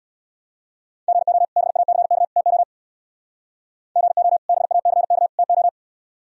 Morse code, heard as a keyed tone, about 64 wpm, 700 Hz. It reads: GO BACK TO GO BACK TO